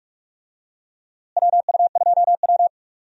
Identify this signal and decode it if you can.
Morse code, heard as a keyed tone, about 46 wpm, 700 Hz. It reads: WU2W
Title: USA Call Signs - 4 Characters - Repeated 3x - 45wpm